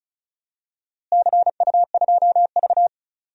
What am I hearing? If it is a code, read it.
Morse code, heard as a keyed tone, about 36 wpm, 700 Hz. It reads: CU2V
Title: DXCC-100 Call Signs - 4 Characters - Repeated 2x - 35wpm